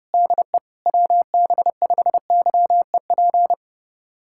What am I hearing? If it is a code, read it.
Morse code, heard as a keyed tone, about 32 wpm, 700 Hz. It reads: DE WB5YEP
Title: USA Callsign DE Callsign - 6 Characters - 30wpm